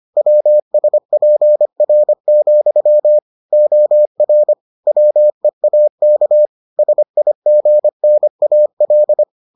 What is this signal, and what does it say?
Morse code, heard as a keyed tone, about 25 wpm, 600 Hz. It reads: WSPR, OR WEAK SIGNAL